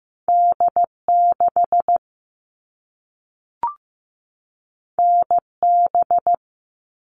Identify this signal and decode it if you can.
Morse code, heard as a keyed tone, about 15 wpm, 700 Hz. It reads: D6 NB
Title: All Cut Numbers - 2-Digit 15wpm